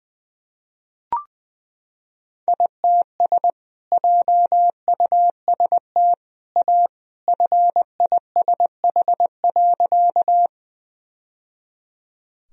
Morse code, heard as a keyed tone, about 20 wpm, 700 Hz. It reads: ITS JUST A FISH.
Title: Sentences from Top 1000 Words - Review Part 1 - 20wpm